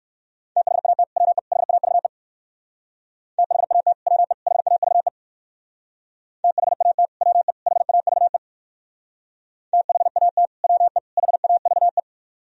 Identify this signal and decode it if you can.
Morse code, heard as a keyed tone, about 58 wpm, 700 Hz. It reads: THAT WE HAVE THAT WE HAVE THAT WE HAVE THAT WE HAVE